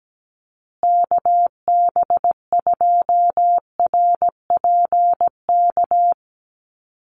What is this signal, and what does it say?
Morse code, heard as a keyed tone, about 17 wpm, 700 Hz. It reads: KB2RPK
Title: USA Call Signs Repeated 2x 17wpm